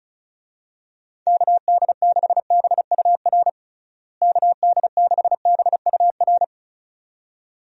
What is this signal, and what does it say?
Morse code, heard as a keyed tone, about 36 wpm, 700 Hz. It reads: KD6BUR KD6BUR
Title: USA Call Signs - 6 Characters - Repeated 2x - 35wpm